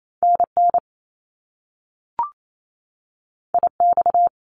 Morse code, heard as a keyed tone, about 28 wpm, 700 Hz. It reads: NN IX
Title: Top Two Letter Combinations in Words 28wpm